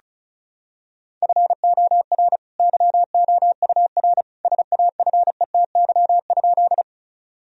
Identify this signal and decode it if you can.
Morse code, heard as a keyed tone, about 34 wpm, 700 Hz. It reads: FOR YOUR SAFETY?
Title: Sentences from Top 1000 Words - Review Part 4 - 35wpm